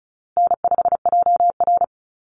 Morse code, heard as a keyed tone, about 34 wpm, 700 Hz. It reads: N5JR
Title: Easy Call Signs Repeated 3x - 35wpm